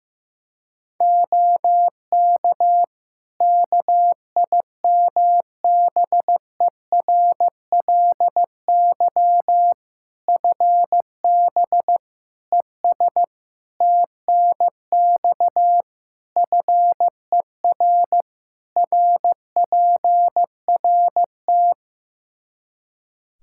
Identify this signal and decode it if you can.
Morse code, heard as a keyed tone, about 16 wpm, 700 Hz. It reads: OK KIMBERLY FB ES TNX FER RPRT
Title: Practice QSO - Exchange 2 - Info - 15wpm